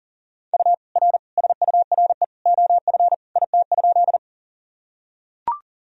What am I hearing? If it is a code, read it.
Morse code, heard as a keyed tone, about 40 wpm, 700 Hz. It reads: U R SURE OF IT?
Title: Sentences from Top 300 Words and Abbreviations 40wpm